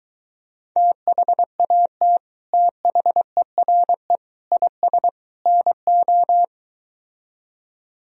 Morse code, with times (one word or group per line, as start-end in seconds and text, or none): 0.8-2.2 THAT
2.5-4.2 THERE
4.5-5.1 IS
5.5-6.4 NO